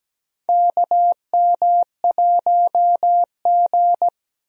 Morse code, heard as a keyed tone, about 17 wpm, 700 Hz. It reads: KM1G
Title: USA Call Signs - 4 Characters - 17wpm